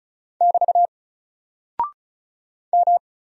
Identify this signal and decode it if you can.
Morse code, heard as a keyed tone, about 36 wpm, 700 Hz. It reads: <BT> M
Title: Single Character (Letter:Number:Punctuation:Prosign) Alt 35wpm